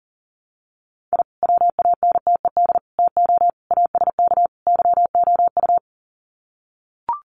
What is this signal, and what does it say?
Morse code, heard as a keyed tone, about 40 wpm, 700 Hz. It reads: I WANTED TO ASK YOU
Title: Sets of 5 Words Encore2 40wpm